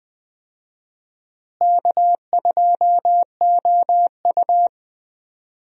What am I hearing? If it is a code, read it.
Morse code, heard as a keyed tone, about 20 wpm, 700 Hz. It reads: K2OU
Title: USA Call Signs - 4 Characters - Repeated 3x - 20wpm